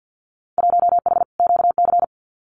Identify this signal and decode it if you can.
Morse code, heard as a keyed tone, about 50 wpm, 700 Hz. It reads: 1H QC
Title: Winter Field Day Exchanges 50wpm